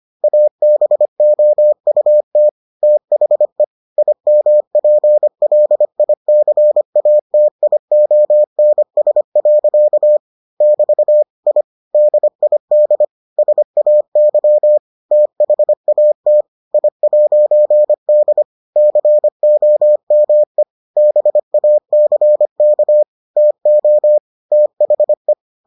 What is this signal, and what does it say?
Morse code, heard as a keyed tone, about 25 wpm, 600 Hz. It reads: ABOUT THE IMPLICATIONS. <BT> I DID SAY THAT I'D COME BACK TO THE